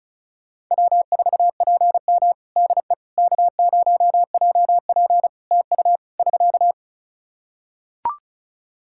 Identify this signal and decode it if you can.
Morse code, heard as a keyed tone, about 34 wpm, 700 Hz. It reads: W4PM DE K0JP TU <SK>
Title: Practice QSO - Exchange 4 - Ending - 35wpm